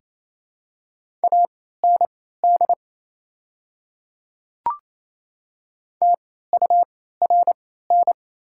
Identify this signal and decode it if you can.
Morse code, heard as a keyed tone, about 28 wpm, spaced as for 16 wpm, 700 Hz. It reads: AND TURN
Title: 3x Character Spacing - Top 300 Words - 28wpm